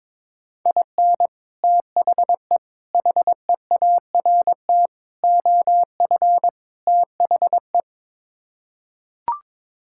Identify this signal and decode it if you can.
Morse code, heard as a keyed tone, about 22 wpm, 700 Hz. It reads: IN THE HEART OF THE
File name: practice / 22wpm / Build A Phrase - 5-Word Phrases - 22wpm